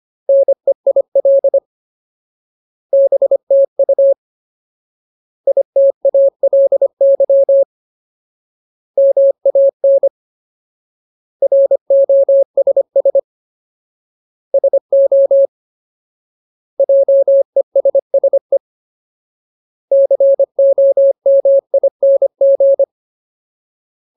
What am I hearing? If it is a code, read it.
Morse code, heard as a keyed tone, about 25 wpm, 550 Hz. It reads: NEIL BTU ITALY MAN ROSS SO JESSE COMING
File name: Ultimate Word List / 25WPM Ultimate